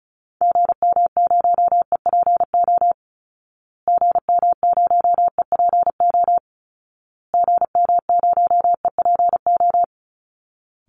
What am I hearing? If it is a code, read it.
Morse code, heard as a keyed tone, about 34 wpm, 700 Hz. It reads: GM0EPO GM0EPO GM0EPO
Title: Call Signs Repeated 2x - 35wpm